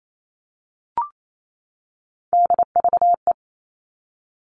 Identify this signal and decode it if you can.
Morse code, heard as a keyed tone, about 28 wpm, 700 Hz. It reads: DVE
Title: All Cut Numbers - 3-Digit 28wpm